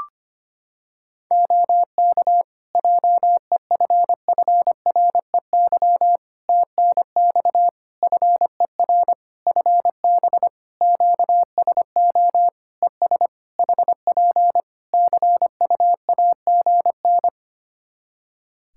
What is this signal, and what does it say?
Morse code, heard as a keyed tone, about 25 wpm, 700 Hz. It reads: OK JEFFREY TNX FER FB QSO ES HP CUAGN
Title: Practice QSO - Exchange 4 - Ending - 25wpm